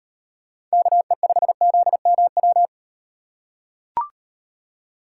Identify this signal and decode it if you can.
Morse code, heard as a keyed tone, about 38 wpm, 700 Hz. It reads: KE5ZMW